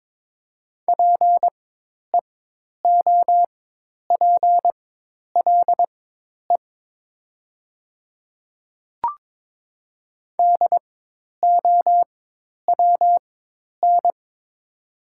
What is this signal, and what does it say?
Morse code, heard as a keyed tone, about 22 wpm, spaced as for 10 wpm, 700 Hz. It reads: PEOPLE DOWN
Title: 4x Character Spacing - Top 100 Words - 22wpm